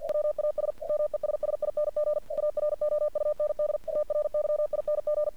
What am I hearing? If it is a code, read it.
Morse code, heard as a keyed tone, about 49 wpm, 600 Hz. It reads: WAS MESSING AROUND TRYING
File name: archived / CRSnet